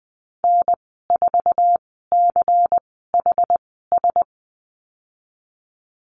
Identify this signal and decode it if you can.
Morse code, heard as a keyed tone, about 20 wpm, spaced as for 15 wpm, 700 Hz. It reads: N4CHS
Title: 128 Lesson 21g - Number 6 - Call Signs 20wpm